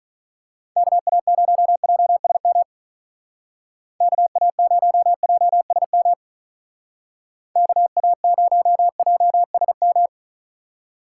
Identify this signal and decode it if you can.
Morse code, heard as a keyed tone, about 43 wpm, 700 Hz. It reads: KA0JSM KA0JSM KA0JSM
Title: Speed Racing - USA 2x3 Call Signs 35wpm